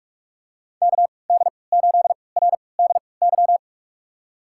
Morse code, heard as a keyed tone, about 46 wpm, spaced as for 25 wpm, 700 Hz. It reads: KD8RDY
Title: USA Call Signs - 6 Characters - 3x Character Spacing - 45wpm